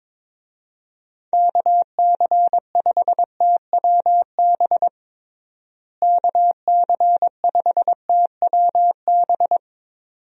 Morse code, heard as a keyed tone, about 22 wpm, 700 Hz. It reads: KC5TWB KC5TWB
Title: USA Call Signs - 6 Characters - Repeated 2x - 22wpm